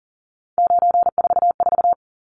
Morse code, heard as a keyed tone, about 40 wpm, 700 Hz. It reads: 944
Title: Numbers - 3-Digit 40wpm